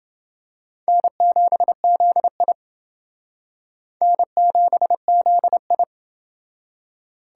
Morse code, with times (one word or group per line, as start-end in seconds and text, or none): 0.9-2.5 N7ZI
4.0-5.8 N7ZI